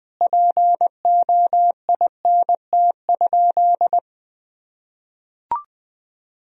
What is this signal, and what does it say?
Morse code, heard as a keyed tone, about 21 wpm, 700 Hz. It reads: POINT?
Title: Sentences from Top 1000 Words - Review Part 1 - 20wpm